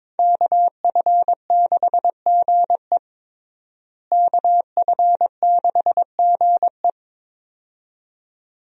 Morse code, heard as a keyed tone, about 22 wpm, 700 Hz. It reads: KF6GE KF6GE
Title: USA Call Signs - 5 Characters - Repeated 2x - 22wpm